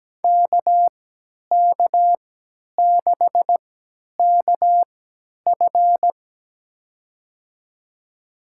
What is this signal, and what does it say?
Morse code, heard as a keyed tone, about 17 wpm, spaced as for 10 wpm, 700 Hz. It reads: KK6KF
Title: USA Call Signs - 5 Characters - 3x Character Spacing - 17wpm